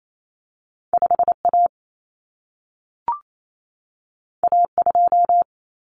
Morse code, heard as a keyed tone, about 28 wpm, 700 Hz. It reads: 5A A2
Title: Common Cut Numbers - 2-Digit 28wpm